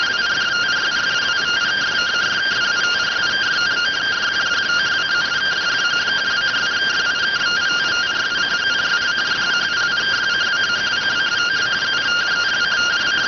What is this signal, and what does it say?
Two-tone teleprinter signal, about 170 Hz shift.